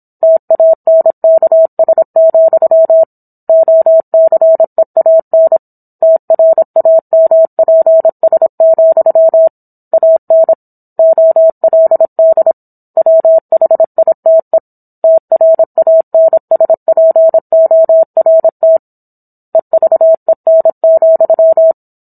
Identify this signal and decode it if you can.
Morse code, heard as a keyed tone, about 27 wpm, 650 Hz. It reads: TANKS, OCEAN TRAMPS, AN OLD WHITE TRANSPORT EVEN,
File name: Book One / 17-Chapter 17 - 26 WPM